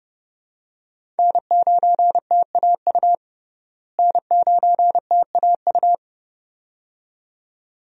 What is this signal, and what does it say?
Morse code, heard as a keyed tone, about 30 wpm, 700 Hz. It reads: N9TAU N9TAU
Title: USA Call Signs - 5 Characters - Repeated 2x - 30wpm